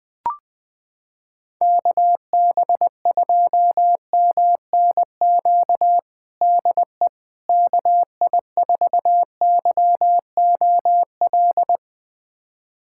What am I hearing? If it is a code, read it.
Morse code, heard as a keyed tone, about 20 wpm, 700 Hz. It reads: KB2MNQ DE KI4YOL
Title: USA Callsign DE Callsign - 6 Characters - 20wpm